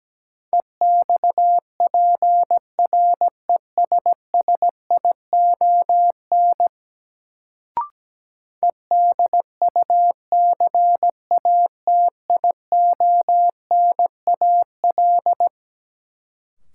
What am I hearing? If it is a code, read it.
Morse code, heard as a keyed tone, about 17 wpm, 700 Hz. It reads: EXPRESSION EDUCATIONAL
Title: Sentences from Top 1200 Words - New Word Drill - 17wpm